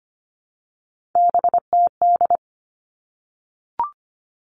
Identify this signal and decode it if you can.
Morse code, heard as a keyed tone, about 25 wpm, 700 Hz. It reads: BTD
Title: All Cut Numbers - 3-Digit 25wpm